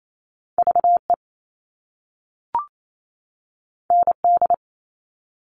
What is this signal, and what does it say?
Morse code, heard as a keyed tone, about 28 wpm, 700 Hz. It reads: VE ND
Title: All Cut Numbers - 2-Digit 28wpm